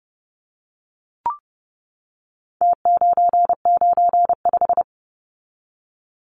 Morse code, heard as a keyed tone, about 30 wpm, 700 Hz. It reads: T995
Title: Numbers - Contest Style - 4-Digit 30wpm